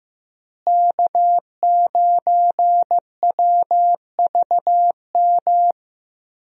Morse code, heard as a keyed tone, about 15 wpm, 700 Hz. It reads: K9WVM
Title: USA Call Signs - 5 Characters - Repeated 3x - 15wpm